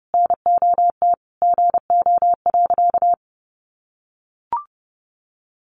Morse code, heard as a keyed tone, about 30 wpm, 700 Hz. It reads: NOT GO.
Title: Sentences from Top 800 Words 30wpm